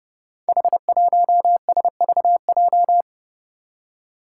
Morse code, heard as a keyed tone, about 31 wpm, 700 Hz. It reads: H1SVJ
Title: Five Challenging Letter-Number Combinations 30wpm